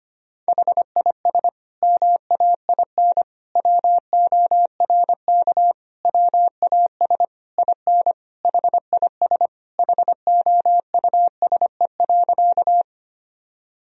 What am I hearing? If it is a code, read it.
Morse code, heard as a keyed tone, about 25 wpm, 700 Hz. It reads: HIS MAIN WORK WAS IN HIS HOUSE.
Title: Sentences from Top 1000 Words - Review Part 2 - 25wpm